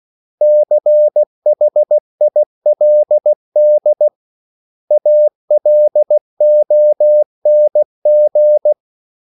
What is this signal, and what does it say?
Morse code, heard as a keyed tone, about 16 wpm, 600 Hz. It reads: CHILD ALONG
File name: CW / Practice / 500MostCommonEnglishWords16